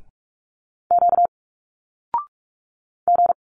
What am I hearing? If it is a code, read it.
Morse code, heard as a keyed tone, about 46 wpm, 700 Hz. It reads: Q G
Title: Single Letters 45wpm